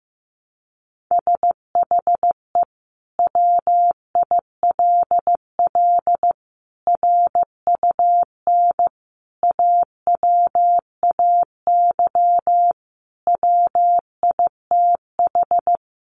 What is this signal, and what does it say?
Morse code, heard as a keyed tone, about 15 wpm, 700 Hz. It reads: SHE WILL RUN AWAY WITH
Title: Sentences from Top 600 Words 15wpm